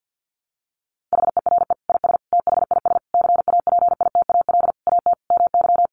Morse code, heard as a keyed tone, about 50 wpm, 700 Hz. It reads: HERE IS THIS CAPITAL AT MY
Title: Sentences from Top 1000 Words 50wpm